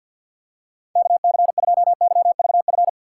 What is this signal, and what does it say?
Morse code, heard as a keyed tone, about 52 wpm, 700 Hz. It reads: KC2YVF
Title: USA Call Signs - 4x Character Spacing - 50wpm